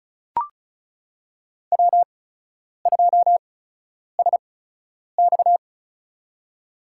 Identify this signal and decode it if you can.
Morse code, heard as a keyed tone, about 36 wpm, spaced as for 10 wpm, 700 Hz. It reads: W2SX
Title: USA Call Signs - 8x Character Spacing - 35wpm